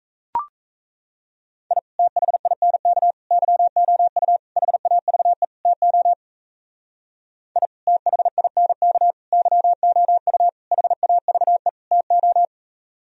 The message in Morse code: I THINK YOU HAVE TO I THINK YOU HAVE TO

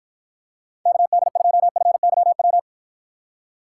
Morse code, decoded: KD2FYW

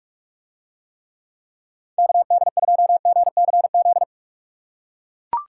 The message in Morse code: KD2GCZ

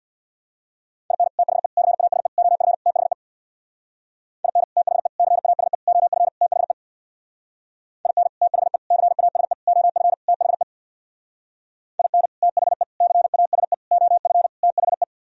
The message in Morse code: IN THE CASE OF THE IN THE CASE OF THE IN THE CASE OF THE IN THE CASE OF THE

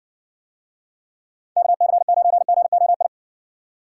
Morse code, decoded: KC9GOI

